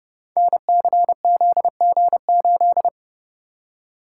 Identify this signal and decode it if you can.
Morse code, heard as a keyed tone, about 31 wpm, 700 Hz. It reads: NCZG8
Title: Code Groups - 5 Letter:Numbers - 30wpm